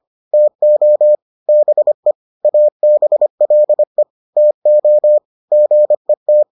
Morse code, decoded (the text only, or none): TO BE ABLE TO GET